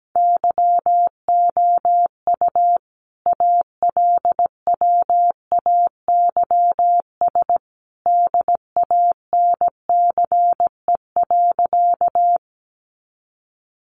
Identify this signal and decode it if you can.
Morse code, heard as a keyed tone, about 17 wpm, 700 Hz. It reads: YOU ALWAYS DANCE.
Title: Sentences from Top 700 Words 17wpm